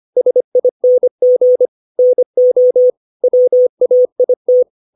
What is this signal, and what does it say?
Morse code, heard as a keyed tone, about 25 wpm, 500 Hz. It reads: SING NO WAIT
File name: 632 words / 632 Words 25WPM